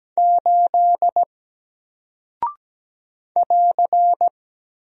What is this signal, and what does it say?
Morse code, heard as a keyed tone, about 17 wpm, 700 Hz. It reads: 8 <AR>
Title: Single Character (Letter:Number:Punctuation:Prosign) Alt 17wpm